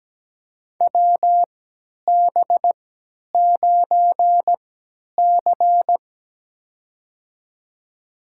Morse code, heard as a keyed tone, about 17 wpm, spaced as for 10 wpm, 700 Hz. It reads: WB9C
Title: USA Call Signs - 4 Characters - 3x Character Spacing - 17wpm